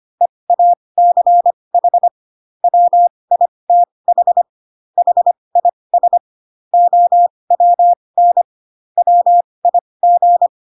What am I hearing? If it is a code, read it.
Morse code, heard as a keyed tone, about 25 wpm, spaced as for 20 wpm, 700 Hz. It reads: EACH WITH HIS OWN WIG
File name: Pinocchio / Pinocchio ch2 15 25 ik1hge